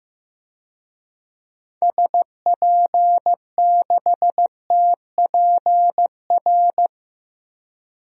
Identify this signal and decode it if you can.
Morse code, heard as a keyed tone, about 15 wpm, 690 Hz. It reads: SP6TPR
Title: DXCC-100 Call Signs - 6 Characters - 15wpm